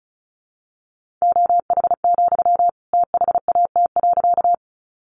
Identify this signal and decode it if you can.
Morse code, heard as a keyed tone, about 34 wpm, 700 Hz. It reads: OH, THAT.
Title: Sentences from Top 500 Words 35wpm